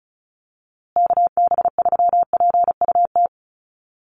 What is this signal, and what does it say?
Morse code, heard as a keyed tone, about 34 wpm, 700 Hz. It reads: KB3PUT
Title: USA Call Signs Repeated 3x 35wpm